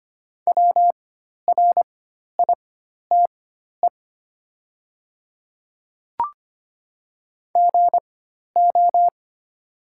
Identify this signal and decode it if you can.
Morse code, heard as a keyed tone, about 25 wpm, spaced as for 12 wpm, 690 Hz. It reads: WRITE GO
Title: 4x Character Spacing - Top 100 Words - 25wpm